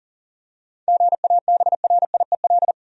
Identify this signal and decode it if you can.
Morse code, heard as a keyed tone, about 38 wpm, 700 Hz. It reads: GABRIEL
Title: USA Common Names - 40wpm